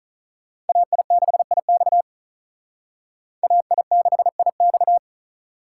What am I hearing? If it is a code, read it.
Morse code, heard as a keyed tone, about 46 wpm, 700 Hz. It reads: AI6IX AI6IX